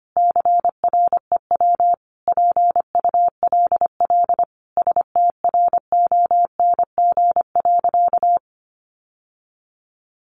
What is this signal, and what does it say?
Morse code, heard as a keyed tone, about 25 wpm, 700 Hz. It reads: CREW PULL STRONG.
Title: Sentences from Top 1500 Words 25wpm